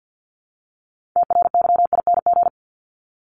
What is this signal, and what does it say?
Morse code, heard as a keyed tone, about 52 wpm, 700 Hz. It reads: TRYING
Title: ING Suffix - Top 100 Words with Base Words 50wpm